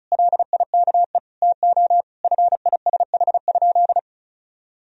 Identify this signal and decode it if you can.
Morse code, heard as a keyed tone, about 34 wpm, 700 Hz. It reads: LIKE TO FISH?
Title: Sentences from Top 1000 Words - Review Part 1 - 35wpm